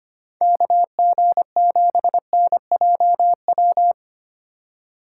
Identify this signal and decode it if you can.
Morse code, heard as a keyed tone, about 25 wpm, 700 Hz. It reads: KG7NJW
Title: USA Call Signs - 6x Character Spacing - 25wpm